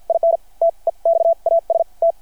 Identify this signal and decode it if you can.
Morse code, heard as a keyed tone, about 53 wpm, spaced as for 40 wpm, 690 Hz. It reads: IN TEXAS T
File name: archived / CRSnet